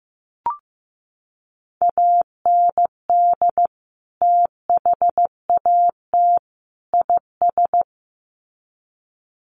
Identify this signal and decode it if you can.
Morse code, heard as a keyed tone, about 15 wpm, 700 Hz. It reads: AND THAT IS
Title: Sets of 3 Words 15wpm